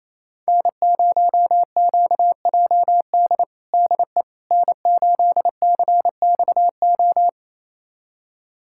Morse code, heard as a keyed tone, about 28 wpm, 700 Hz. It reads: N0QJD DE N8CXO
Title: USA Callsign DE Callsign - 5 Characters - 28wpm